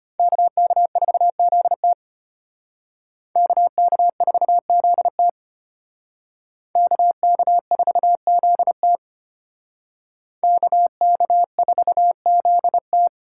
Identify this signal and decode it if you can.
Morse code, heard as a keyed tone, about 33 wpm, 700 Hz. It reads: KK4ZT KK4ZT KK4ZT KK4ZT